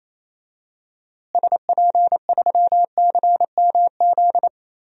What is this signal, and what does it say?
Morse code, heard as a keyed tone, about 28 wpm, 700 Hz. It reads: SP3CMZ
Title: DXCC-100 Call Signs - 6 Characters - Repeated 2x - 28wpm